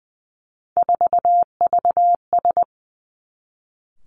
Morse code, heard as a keyed tone, about 20 wpm, 700 Hz. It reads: 4VS
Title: Three Challenging Letter-Number Combinations 20wpm